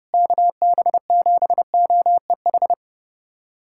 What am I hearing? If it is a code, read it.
Morse code, heard as a keyed tone, about 30 wpm, 700 Hz. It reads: KB7OEH
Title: USA Callsign DE Callsign - 6 Characters - 30wpm